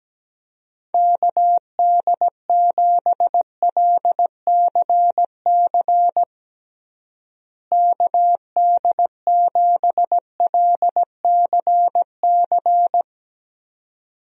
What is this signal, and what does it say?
Morse code, heard as a keyed tone, about 17 wpm, 700 Hz. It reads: KD7LCC KD7LCC
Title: USA Call Signs - 6 Characters - Repeated 2x - 17wpm